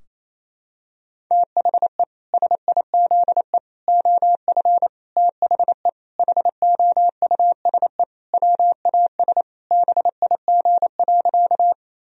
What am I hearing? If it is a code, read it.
Morse code, heard as a keyed tone, about 27 wpm, 700 Hz. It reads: THE SIZE OF THE HOUSE WAS BIG.
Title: Sentences from Top 1000 Words - Review Part 2 - 28wpm